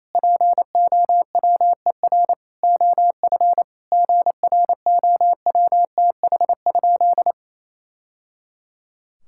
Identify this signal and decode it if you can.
Morse code, heard as a keyed tone, about 27 wpm, 700 Hz. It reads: POWER OF GROWTH?
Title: Sentences from Top 1000 Words - Review Part 5 - 28wpm